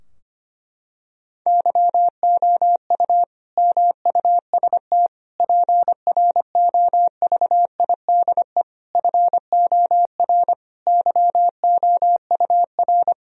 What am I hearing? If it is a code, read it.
Morse code, heard as a keyed tone, about 25 wpm, 700 Hz. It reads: YOU MUST PROVIDE FOR YOUR
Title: Sentences from Top 1000 Words - Review Part 3 - 25wpm